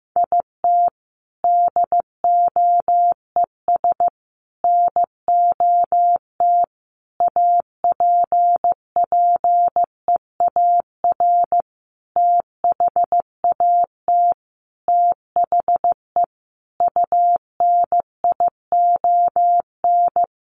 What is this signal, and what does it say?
Morse code, heard as a keyed tone, about 15 wpm, 700 Hz. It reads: IT DOES NOT APPEAR THAT THE UNION
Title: Sentences from Top 1400 Words 15wpm